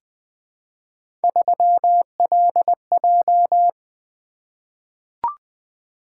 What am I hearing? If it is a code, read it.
Morse code, heard as a keyed tone, about 20 wpm, 700 Hz. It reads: 3LJ
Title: Three Letter-Number Combinations - 20wpm